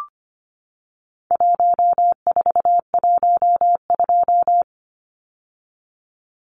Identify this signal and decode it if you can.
Morse code, heard as a keyed tone, about 25 wpm, 700 Hz. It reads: 1412
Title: Numbers - 4-Digit 25wpm